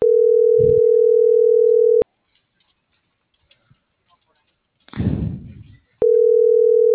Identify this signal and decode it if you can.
Ringback tone (North America), 440 + 480 Hz.